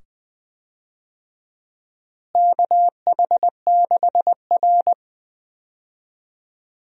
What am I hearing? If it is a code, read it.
Morse code, heard as a keyed tone, about 20 wpm, 700 Hz. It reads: KH6R
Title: DXCC-100 Call Signs - 4 Characters - Repeated 2x - 20wpm